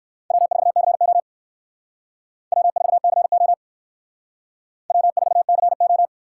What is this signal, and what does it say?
Morse code, heard as a keyed tone, about 56 wpm, 700 Hz. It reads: W4CO W4CO W4CO
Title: Speed Racing - USA 1x2 Call Signs 45wpm